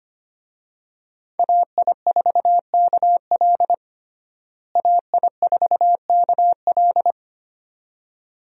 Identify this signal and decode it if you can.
Morse code, heard as a keyed tone, about 25 wpm, 700 Hz. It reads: AI4KL AI4KL
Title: USA Call Signs - 5 Characters - Repeated 2x - 25wpm